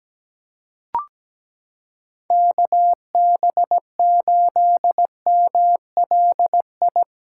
Morse code, heard as a keyed tone, about 17 wpm, 700 Hz. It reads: KB8MLI
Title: USA Callsign DE Callsign 17wpm